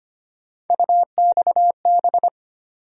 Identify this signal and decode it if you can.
Morse code, heard as a keyed tone, about 25 wpm, 700 Hz. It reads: UXB
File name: Three Letter-Number Combinations - 25wpm